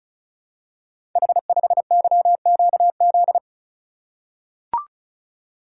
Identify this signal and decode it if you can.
Morse code, heard as a keyed tone, about 36 wpm, 700 Hz. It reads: H5YQZ
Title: Code Groups - 5 Letter:Numbers - 35wpm